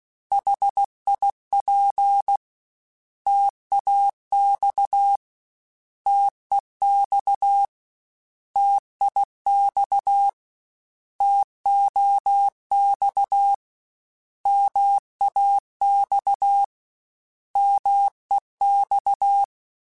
Morse code, heard as a keyed tone, about 16 wpm, 800 Hz. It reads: HIP TAX TEX TIX TOX MAX MEX
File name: Contrasts / Contrasts-mp3s / 3LtrContrasts @15